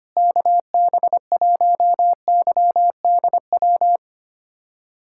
Morse code, heard as a keyed tone, about 25 wpm, 700 Hz. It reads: KB1YDW
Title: USA Callsign DE Callsign - 6 Characters - 25wpm